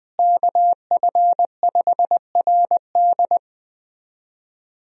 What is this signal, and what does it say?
Morse code, heard as a keyed tone, about 20 wpm, 700 Hz. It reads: KF5RD
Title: USA Call Signs - 5 Characters - Repeated 2x - 20wpm